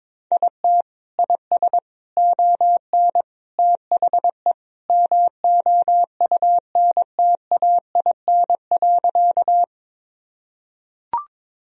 Morse code, heard as a keyed tone, about 22 wpm, 700 Hz. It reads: IT IS ON THE MOUNTAIN.
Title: Sentences from Top 1000 Words - Review Part 1 - 22wpm